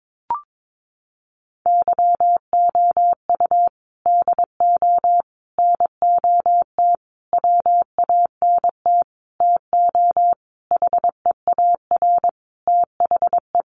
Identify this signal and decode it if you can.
Morse code, heard as a keyed tone, about 22 wpm, 700 Hz. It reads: YOU DO NOT WANT TO HEAR THE
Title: Sentences from Top 1400 Words 22wpm